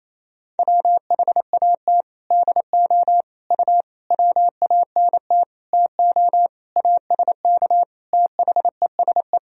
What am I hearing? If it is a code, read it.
Morse code, heard as a keyed tone, about 28 wpm, 700 Hz. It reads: WHAT DO U WANT TO ASK THESE
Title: Sentences from Top 200 Words and Abbreviations 28wpm